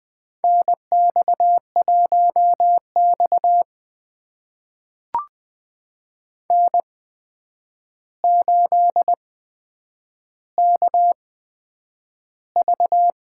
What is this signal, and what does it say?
Morse code, heard as a keyed tone, about 20 wpm, spaced as for 9 wpm, 700 Hz. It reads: NX1X N8KV
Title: USA Call Signs - 4 Characters - 8x Character Spacing - 20wpm